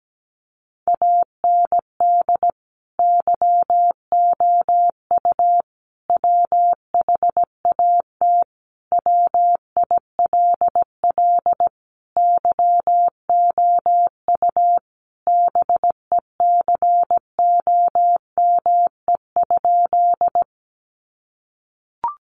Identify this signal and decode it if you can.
Morse code, heard as a keyed tone, about 17 wpm, 700 Hz. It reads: AND YOU WHAT WILL YOU BECOME?
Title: Sentences from Top 600 Words 17wpm